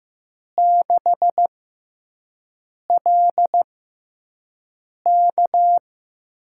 Morse code, heard as a keyed tone, about 15 wpm, spaced as for 5 wpm, 700 Hz. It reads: 6LK
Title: USA Call Signs - 5 Characters - 6x Character Spacing - 15wpm